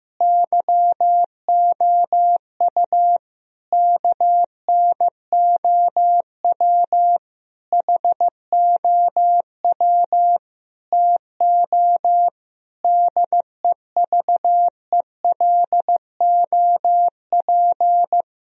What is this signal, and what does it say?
Morse code, heard as a keyed tone, about 16 wpm, 700 Hz. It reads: YOU KNOW HOW TO DEVELOP
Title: Sentences from Top 1000 Words - Review Part 3 - 15wpm